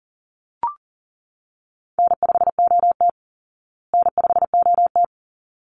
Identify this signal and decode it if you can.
Morse code, heard as a keyed tone, about 37 wpm, 700 Hz. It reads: N5OT N5OT
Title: Easy Call Signs Repeated 2x - 40wpm